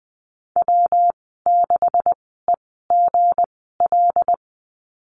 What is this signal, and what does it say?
Morse code, heard as a keyed tone, about 20 wpm, spaced as for 15 wpm, 700 Hz. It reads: W6EGL
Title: 167 Lesson 27g - Number 7 - Call Signs 20wpm